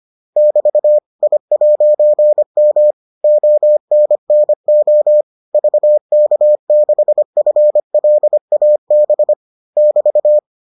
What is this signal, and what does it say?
Morse code, heard as a keyed tone, about 25 wpm, 600 Hz. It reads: <BT> I'M ONNO VK6FLAB <BT>